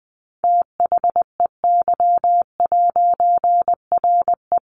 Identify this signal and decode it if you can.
Morse code, heard as a keyed tone, about 20 wpm, 700 Hz. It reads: THEY'RE
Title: A Sound Of Thunder 20wpm